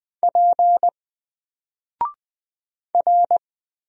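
Morse code, heard as a keyed tone, about 20 wpm, 700 Hz. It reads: P R
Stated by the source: Single letter and number Morse drill at 20 wpm, from N4LCD.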